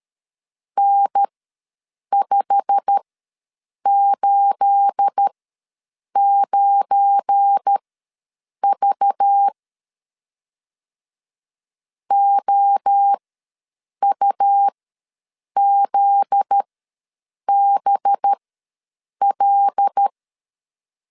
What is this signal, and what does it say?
Morse code, heard as a keyed tone, about 13 wpm, spaced as for 7 wpm, 800 Hz. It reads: N589V OUZBL
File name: LEZIONI / 9 28 ws 12wpm 3354